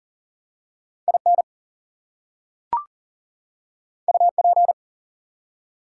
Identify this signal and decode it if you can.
Morse code, heard as a keyed tone, about 40 wpm, 700 Hz. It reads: IN UP
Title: Two Letter Words - 40wpm